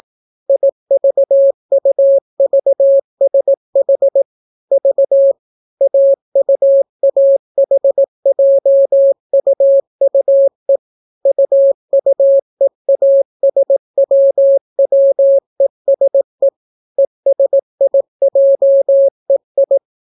Morse code, heard as a keyed tone, about 18 wpm, 550 Hz. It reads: IVUVSH V AUAHJUUE UUEASWWESE ESIJEI